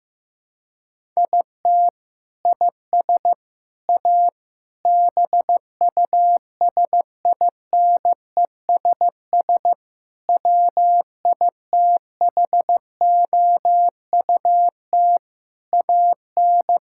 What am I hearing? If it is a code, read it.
Morse code, heard as a keyed tone, about 15 wpm, 700 Hz. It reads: IT IS A BUSINESS WITHOUT AN
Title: Sentences from Top 1600 Words 15wpm